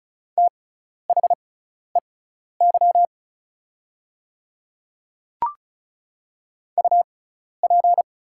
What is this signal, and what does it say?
Morse code, heard as a keyed tone, about 36 wpm, spaced as for 12 wpm, 690 Hz. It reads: THEY UP
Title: 6x Character Spacing - Top 500 Words - 35wpm